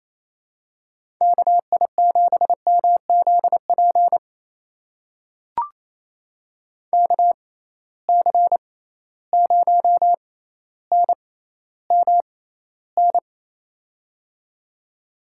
Morse code, heard as a keyed tone, about 28 wpm, spaced as for 14 wpm, 700 Hz. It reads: KI7MZP KC0NMN
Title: USA Call Signs - 6 Characters - 6x Character Spacing - 28wpm